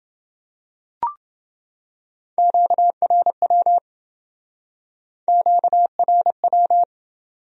Morse code, heard as a keyed tone, about 28 wpm, 700 Hz. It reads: QRW QRW